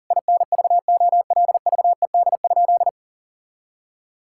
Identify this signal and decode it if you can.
Morse code, heard as a keyed tone, about 40 wpm, 700 Hz. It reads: INVOLVED?
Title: Sentences from Top 1000 Words - Review Part 5 - 40wpm